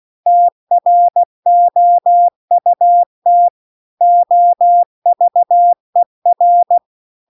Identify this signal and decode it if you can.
Morse code, heard as a keyed tone, about 16 wpm, 700 Hz. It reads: TROUT OVER
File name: CW-The-Big-Twoheaded-River-2-16WPM